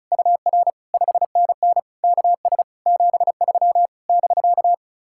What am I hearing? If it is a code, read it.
Morse code, heard as a keyed tone, about 43 wpm, 700 Hz. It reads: UR 5NN KS 73 <BK>
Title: Speed-Racing Common QSO Phrases - 35wpm